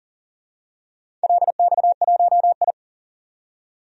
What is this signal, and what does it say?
Morse code, heard as a keyed tone, about 40 wpm, 700 Hz. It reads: LX1I
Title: DXCC-100 Call Signs - 4 Characters - 40wpm